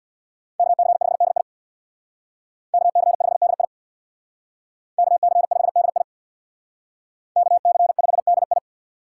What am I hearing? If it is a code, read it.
Morse code, heard as a keyed tone, about 57 wpm, 700 Hz. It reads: KC5DI KC5DI KC5DI KC5DI
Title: Speed Racing - USA 2x2 Call Signs 50wpm